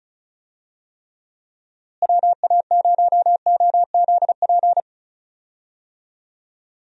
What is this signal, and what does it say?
Morse code, heard as a keyed tone, about 36 wpm, 700 Hz. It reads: WA0OZP
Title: USA Call Signs - 6 Characters - Repeated 2x - 35wpm